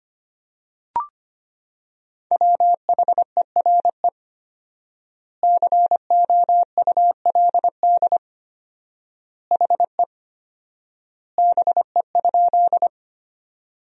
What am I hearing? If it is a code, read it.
Morse code, heard as a keyed tone, about 25 wpm, 700 Hz. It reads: WHERE COULD HE BE?
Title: 4x Word-Spacing - Sentences from Top 200 Words - 25wpm